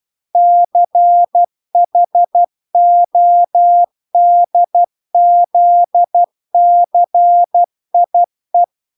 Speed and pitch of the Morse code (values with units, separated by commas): 12 wpm, 700 Hz